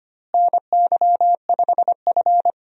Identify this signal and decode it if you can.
Morse code, heard as a keyed tone, about 25 wpm, 700 Hz. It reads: NY5F
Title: USA Call Signs - 4 Characters - Repeated 3x - 25wpm